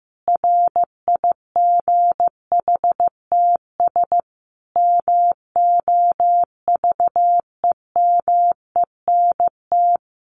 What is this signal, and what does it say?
Morse code, heard as a keyed tone, about 15 wpm, 700 Hz. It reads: RIGHTS MOVEMENT
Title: Sets of 5 Words Encore2 15wpm